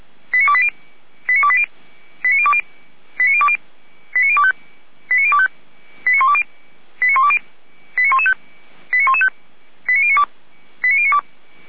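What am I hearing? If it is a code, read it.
Five-tone selective call (ZVEI), heard as quick runs of five tones, tones of about 70 ms. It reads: 89280 89280 89020 89020 89025 89025 89120 89120 89115 89115 89092 89092
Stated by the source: Five-tone ZVEI selective call received on SDR, from the audio server of Feuerwehr Orlamünde (fire brigade).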